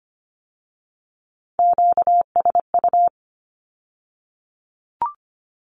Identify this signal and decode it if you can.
Morse code, heard as a keyed tone, about 25 wpm, 700 Hz. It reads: QSU
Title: Q-Codes with Spoken Meaning - 25wpm